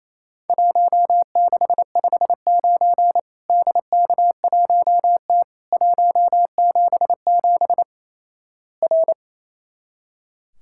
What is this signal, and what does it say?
Morse code, heard as a keyed tone, about 28 wpm, 700 Hz. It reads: 1659 DK1T 177 R
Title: Worked All Europe DX Contest - QTC Exchange - 28wpm